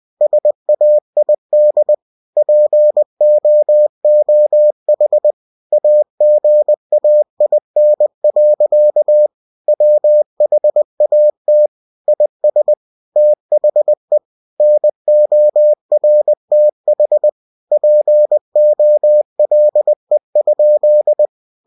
Morse code, heard as a keyed tone, about 20 wpm, 600 Hz. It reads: SAID POOH AGAIN. WHAT IS THE NORTH POLE?